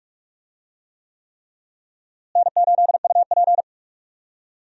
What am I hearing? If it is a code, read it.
Morse code, heard as a keyed tone, about 46 wpm, 700 Hz. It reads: N8UP
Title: USA Call Signs - 4 Characters - Repeated 3x - 45wpm